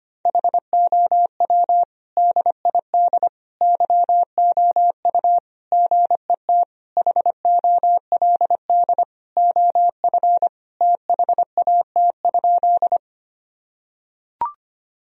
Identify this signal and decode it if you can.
Morse code, heard as a keyed tone, about 25 wpm, 700 Hz. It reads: HOW DID YOU GET HOLD OF THAT?
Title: Sentences from Top 1000 Words - Review Part 2 - 25wpm